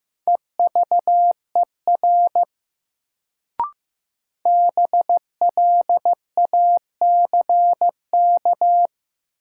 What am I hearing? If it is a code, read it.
Morse code, heard as a keyed tone, about 15 wpm, 700 Hz. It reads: EVER BLACK
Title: Sentences from Top 400 Words - New Word Drill - 15wpm